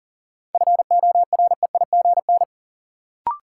Morse code, heard as a keyed tone, about 40 wpm, 700 Hz. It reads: FOREIGN
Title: Sentences from Top 1400 Words - New Word Drill - 40wpm